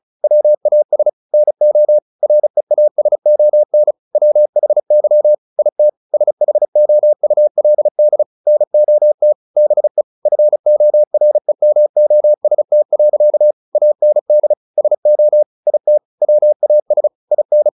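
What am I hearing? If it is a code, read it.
Morse code, heard as a keyed tone, about 34 wpm, 600 Hz. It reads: WAS NO REASON WHY IT SHOULD NOT BE FOREMOST. AND SO IT WAS IN
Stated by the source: Morse practice at 35 wpm from CW med Gnister, sending Ken Follett's novel Never in English.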